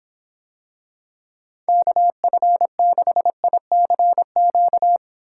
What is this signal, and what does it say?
Morse code, heard as a keyed tone, about 26 wpm, 700 Hz. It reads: KF6ICQ